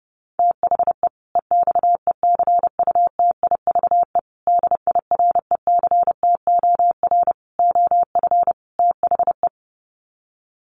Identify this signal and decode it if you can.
Morse code, heard as a keyed tone, about 30 wpm, 700 Hz. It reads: THE EXECUTIVE DIRECTOR OF THE
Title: Sets of 5 Words Encore 30wpm